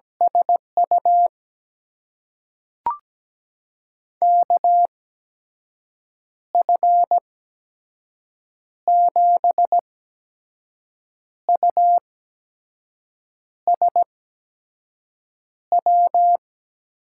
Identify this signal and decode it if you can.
Morse code, heard as a keyed tone, about 17 wpm, spaced as for 5 wpm, 700 Hz. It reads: SU KF7USW
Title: USA Call Signs - 6 Characters - 8x Character Spacing - 17wpm